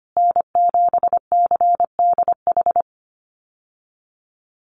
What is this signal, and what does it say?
Morse code, heard as a keyed tone, about 25 wpm, 700 Hz. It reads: N7CDH
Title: USA Call Signs Repeated 3x 25wpm